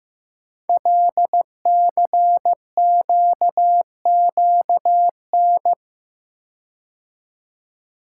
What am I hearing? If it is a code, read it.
Morse code, heard as a keyed tone, about 15 wpm, 700 Hz. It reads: LCQQN
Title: Code Groups - 5 Letters - 15wpm